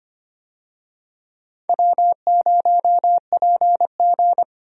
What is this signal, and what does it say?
Morse code, heard as a keyed tone, about 25 wpm, 700 Hz. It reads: W0PG
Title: USA Call Signs - 4 Characters - 25wpm